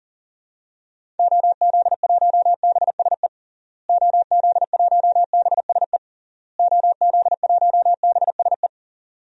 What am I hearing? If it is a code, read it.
Morse code, heard as a keyed tone, about 37 wpm, 700 Hz. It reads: OZ1BSE OZ1BSE OZ1BSE
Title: DXCC-100 Call Signs - 6 Characters - Repeated 3x - 40wpm